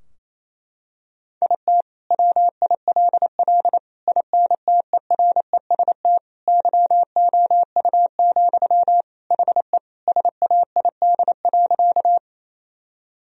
Morse code, heard as a keyed tone, about 27 wpm, 700 Hz. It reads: IT WILL INTEREST YOU, HE SAID.
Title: Sentences from Top 1000 Words - Review Part 2 - 28wpm